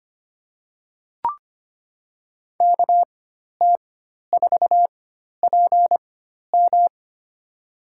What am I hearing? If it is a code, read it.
Morse code, heard as a keyed tone, about 25 wpm, spaced as for 12 wpm, 700 Hz. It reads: KT4PM
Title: USA Call Signs - 5 Characters - 4x Character Spacing - 25wpm